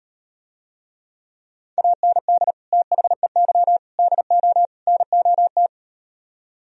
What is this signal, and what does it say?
Morse code, heard as a keyed tone, about 38 wpm, 700 Hz. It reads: AND THEY DO NOT